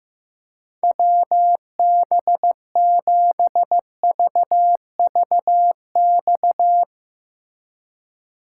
Morse code, heard as a keyed tone, about 17 wpm, 700 Hz. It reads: WB7VVX
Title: Speed Racing - USA 2x3 Call Signs 15wpm